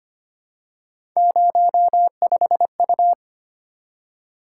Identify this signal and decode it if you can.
Morse code, heard as a keyed tone, about 25 wpm, 700 Hz. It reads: 05U
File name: Three Letter-Number Combinations - 25wpm